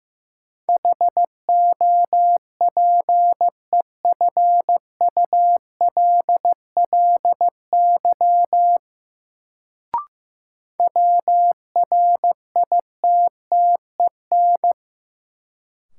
Morse code, heard as a keyed tone, about 16 wpm, 700 Hz. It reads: HOPEFULLY WRITTEN
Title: Sentences from Top 1500 Words - New Word Drill - 15wpm